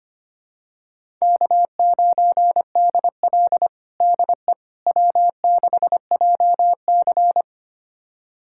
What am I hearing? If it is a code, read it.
Morse code, heard as a keyed tone, about 25 wpm, 700 Hz. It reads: K9DL DE W6JC
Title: USA Callsign DE Callsign - 4 Characters - 25wpm